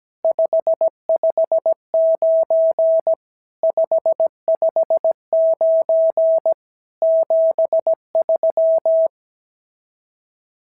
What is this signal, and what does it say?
Morse code, heard as a keyed tone, about 16 wpm, 650 Hz. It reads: 559 559 73
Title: SOTA QSO Practice 17wpm